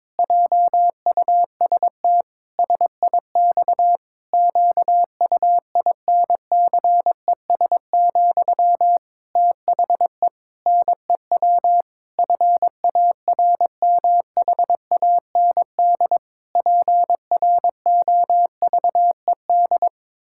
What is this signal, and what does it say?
Morse code, heard as a keyed tone, about 22 wpm, 700 Hz. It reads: JUST SIX QUINCES, THE NEW FARMHAND PROVED
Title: Pangram Challenge v2 - 22wpm